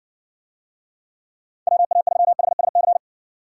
Morse code, heard as a keyed tone, about 59 wpm, 700 Hz. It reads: WA3HSC